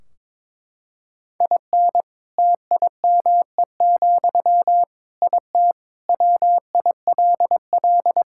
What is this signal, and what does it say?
Morse code, heard as a keyed tone, about 22 wpm, 700 Hz. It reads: IN TIME, IT WILL
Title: Sentences from Top 1600 Words 22wpm